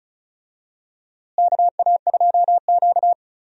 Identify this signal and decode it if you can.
Morse code, heard as a keyed tone, about 36 wpm, 700 Hz. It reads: KA2Q
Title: USA Call Signs - 4 Characters - Repeated 2x - 35wpm